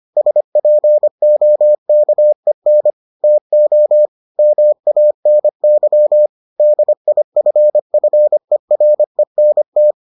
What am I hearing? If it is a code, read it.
Morse code, heard as a keyed tone, about 25 wpm, 600 Hz. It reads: SPOKEN TO MANY DIFFERENT